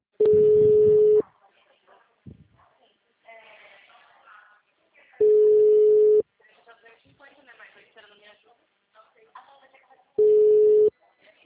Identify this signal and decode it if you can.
Ringback tone (Europe, 425 Hz), with speech.